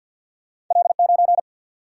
Morse code, heard as a keyed tone, about 53 wpm, 700 Hz. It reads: L 9
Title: Single-Letter Single-Number 50wpm